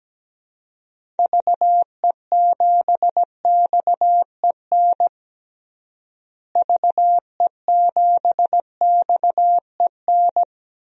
Morse code, heard as a keyed tone, about 17 wpm, 700 Hz. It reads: VE7XEN VE7XEN
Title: DXCC-100 Call Signs - 6 Characters - Repeated 2x - 17wpm